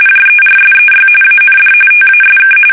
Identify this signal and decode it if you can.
RTTY, about 45.45 baud. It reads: JERSEY AND TASK
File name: fsk850 / battleshipnj